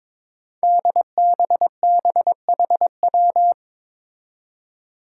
Morse code, heard as a keyed tone, about 22 wpm, 700 Hz. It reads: DBBHW
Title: Five Challenging Letter-Number Combinations 22wpm